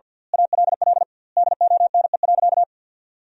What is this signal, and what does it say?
Morse code, heard as a keyed tone, about 47 wpm, 700 Hz. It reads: ALL DONE.
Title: 2x Word-Spacing - Sentences from Top 500 Words - 50wpm